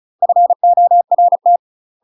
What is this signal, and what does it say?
Morse code, heard as a keyed tone, about 36 wpm, 700 Hz. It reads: FORT